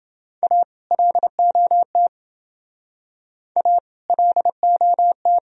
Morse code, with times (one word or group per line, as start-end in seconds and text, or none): 0.4-0.6 A
0.9-2.1 LOT
3.6-3.8 A
4.1-5.4 LOT